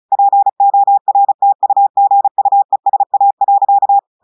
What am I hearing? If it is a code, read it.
Morse code, heard as a keyed tone, about 34 wpm, 800 Hz. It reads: PORTUGUESA.
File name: audios / portugais / portugais texte en clair 1 vitesse 35 mots minutes